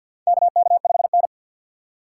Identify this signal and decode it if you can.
Morse code, heard as a keyed tone, about 53 wpm, 700 Hz. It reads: KK5N
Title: USA Call Signs - 2x Character Spacing - 50wpm